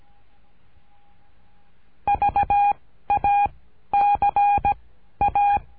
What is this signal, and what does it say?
Morse code, heard as a keyed tone, about 17 wpm, 810 Hz. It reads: V A C A
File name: cursos / CURSO DE TELEGRAFIA / Telegrafia - Trilha 13